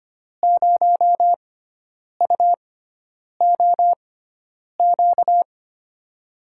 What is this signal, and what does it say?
Morse code, heard as a keyed tone, about 25 wpm, spaced as for 9 wpm, 700 Hz. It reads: 0UOQ
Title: USA Call Signs - 6x Character Spacing - 25wpm